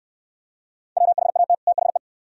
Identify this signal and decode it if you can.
Morse code, heard as a keyed tone, about 69 wpm, 700 Hz. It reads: WHAT THE